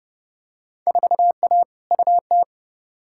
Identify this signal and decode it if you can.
Morse code, heard as a keyed tone, about 31 wpm, 700 Hz. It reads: 4A UT
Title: Field Day Exchanges - 30wpm